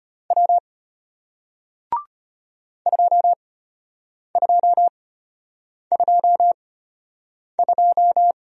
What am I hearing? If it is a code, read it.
Morse code, heard as a keyed tone, about 34 wpm, spaced as for 8 wpm, 690 Hz. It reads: W 2222